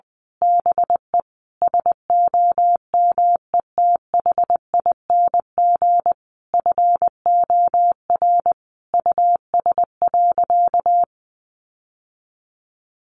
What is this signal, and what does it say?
Morse code, heard as a keyed tone, about 20 wpm, 700 Hz. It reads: BE SOMETHING FOR US.
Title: Sentences from Top 1200 Words 20wpm